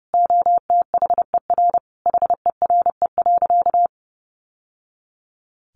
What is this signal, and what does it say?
Morse code, heard as a keyed tone, about 30 wpm, 700 Hz. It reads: OTHER HERE.
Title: Sentences from Top 1200 Words 30wpm